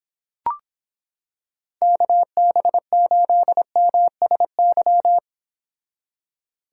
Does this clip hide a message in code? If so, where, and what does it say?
Yes: Morse code from 1.8 to 5.2 s, heard as a keyed tone, about 26 wpm, 700 Hz. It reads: KB8MSY